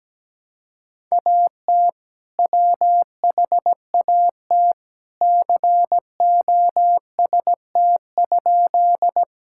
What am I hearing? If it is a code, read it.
Morse code, heard as a keyed tone, about 17 wpm, 700 Hz. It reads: AT WHAT COST?
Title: Sentences from Top 1000 Words - Review Part 5 - 17wpm